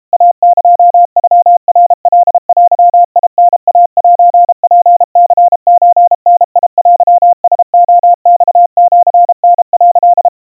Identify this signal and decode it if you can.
Morse code (Japanese Wabun code), heard as a keyed tone, about 32 wpm, 700 Hz. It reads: イエノナカデタイセツニソダテラレマシタ」
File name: kawa / Morse sample / kamyu 120 cpm 700 Hz Crispy